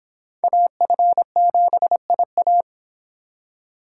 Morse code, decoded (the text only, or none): AF7IA